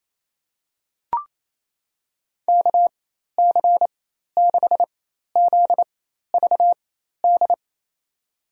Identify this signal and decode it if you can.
Morse code, heard as a keyed tone, about 27 wpm, spaced as for 13 wpm, 700 Hz. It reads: KC6ZVD
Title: USA Call Signs - 6 Characters - 4x Character Spacing - 28wpm